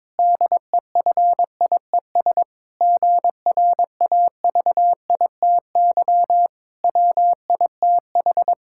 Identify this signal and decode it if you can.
Morse code, heard as a keyed tone, about 22 wpm, 700 Hz. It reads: DEFIES GRAVITY WITH
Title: Pangram Challenge v2 - 22wpm